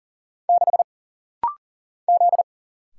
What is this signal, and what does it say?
Morse code, heard as a keyed tone, about 40 wpm, 700 Hz. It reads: 6 Z
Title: Single Letter-Number - Rapid-Fire 40wpm